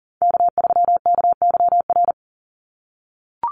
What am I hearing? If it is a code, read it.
Morse code, heard as a keyed tone, about 40 wpm, 700 Hz. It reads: K3KYR
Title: NA SOTA Chaser Callsigns 40wpm